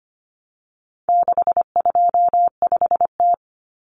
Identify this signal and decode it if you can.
Morse code, heard as a keyed tone, about 25 wpm, 700 Hz. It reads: 625T
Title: Common Cut Numbers - 4-Digit 25wpm